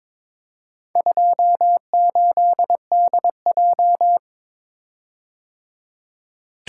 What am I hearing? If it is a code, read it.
Morse code, heard as a keyed tone, about 22 wpm, 700 Hz. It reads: 28DJ
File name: practice / 22wpm / Four Letter-Number Combinations - 22wpm